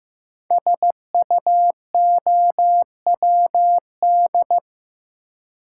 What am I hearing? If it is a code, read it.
Morse code, heard as a keyed tone, about 15 wpm, 700 Hz. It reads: SUOWD
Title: Code Groups - 5 Letters - 15wpm